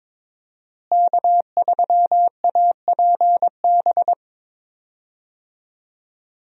Morse code, heard as a keyed tone, about 22 wpm, 700 Hz. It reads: K3APB
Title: USA Call Signs - 5 Characters - Repeated 2x - 22wpm